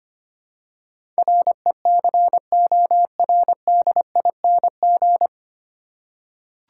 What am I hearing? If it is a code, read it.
Morse code, heard as a keyed tone, about 25 wpm, 700 Hz. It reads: RECORDING
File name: Nine Letter Words - 25wpm